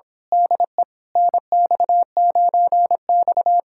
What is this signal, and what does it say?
Morse code, heard as a keyed tone, about 26 wpm, 700 Hz. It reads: DE NX9X
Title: Speed-Racing Common QSO Phrases Encore - 22wpm